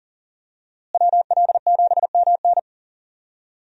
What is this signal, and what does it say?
Morse code, heard as a keyed tone, about 40 wpm, 700 Hz. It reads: WL7MN
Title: USA Call Signs - 5 Characters - Repeated 3x - 40wpm